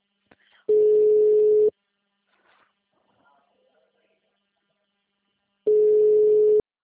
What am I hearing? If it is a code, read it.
Ringback tone (Europe, 425 Hz).